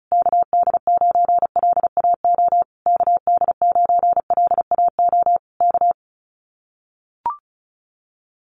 Morse code, heard as a keyed tone, about 34 wpm, 700 Hz. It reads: KD9LAO KD9LAO K
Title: Common QSO Phrases Encore 35wpm